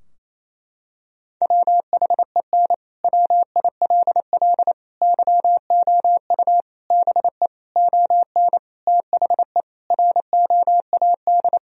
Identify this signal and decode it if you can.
Morse code, heard as a keyed tone, about 28 wpm, 700 Hz. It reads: WHEN WILL YOU BE ON THE ROAD
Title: Sentences from Top 1000 Words - Review Part 2 - 28wpm